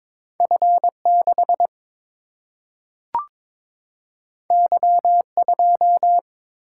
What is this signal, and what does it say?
Morse code, heard as a keyed tone, about 22 wpm, 700 Hz. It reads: F 6 Y 2
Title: Single-Letter Single-Number 22wpm